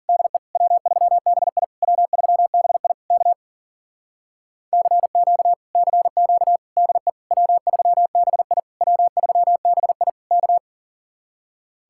Morse code, heard as a keyed tone, about 50 wpm, 700 Hz. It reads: DE W3BI W3BI K CQ CQ DE W3BI W3BI K
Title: Speed-Racing Common QSO Phrases - 40wpm